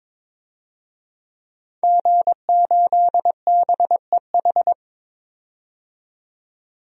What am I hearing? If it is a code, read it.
Morse code, heard as a keyed tone, about 23 wpm, 700 Hz. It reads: G8BEH
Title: DXCC-100 Call Signs - 5 Characters - 22wpm